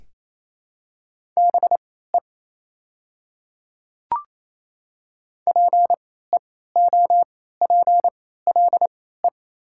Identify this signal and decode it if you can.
Morse code, heard as a keyed tone, about 28 wpm, spaced as for 16 wpm, 690 Hz. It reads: BE PEOPLE
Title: 3x Character Spacing - Top 100 Words - 28wpm